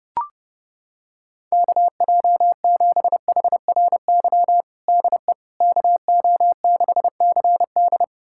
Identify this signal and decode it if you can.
Morse code, heard as a keyed tone, about 30 wpm, 700 Hz. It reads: KJ7HRY DE KO6CD
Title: USA Callsign DE Callsign 30wpm